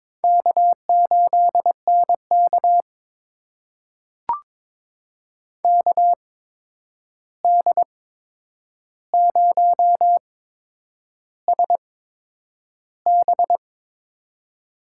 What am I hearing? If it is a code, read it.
Morse code, heard as a keyed tone, about 22 wpm, spaced as for 9 wpm, 700 Hz. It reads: K8NK KD0SB
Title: USA Call Signs - 8x Character Spacing - 22wpm